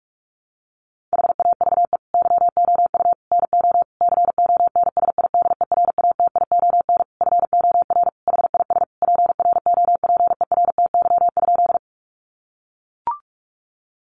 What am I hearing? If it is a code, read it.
Morse code, heard as a keyed tone, about 44 wpm, 700 Hz. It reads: HAVE YOU NO CONSIDERATION FOR HIS PROPERTY?
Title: Sentences from Top 1500 Words 45wpm